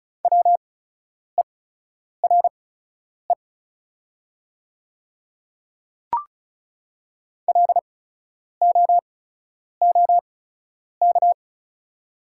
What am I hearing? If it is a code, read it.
Morse code, heard as a keyed tone, about 36 wpm, spaced as for 10 wpm, 700 Hz. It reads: WERE LOOK
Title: 8x Character Spacing - Top 100 Words - 35wpm